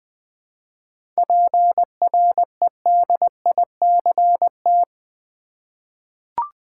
Morse code, heard as a keyed tone, about 20 wpm, 690 Hz. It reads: PREDICT
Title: 7 Letter Words - First 3 Letters Given - 20wpm